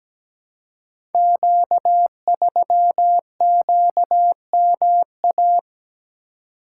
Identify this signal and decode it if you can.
Morse code, heard as a keyed tone, about 17 wpm, 700 Hz. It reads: Q3QMA
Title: Code Groups - 5 Letter:Numbers - 17wpm